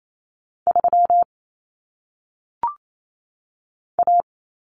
Morse code, heard as a keyed tone, about 28 wpm, 700 Hz. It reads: 3 A
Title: Common Cut Numbers - 1-Digit 28wpm